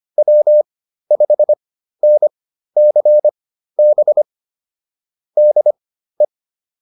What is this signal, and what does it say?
Morse code, heard as a keyed tone, about 25 wpm, spaced as for 13 wpm, 600 Hz. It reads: W5NCB DE
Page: intermediate-practice-files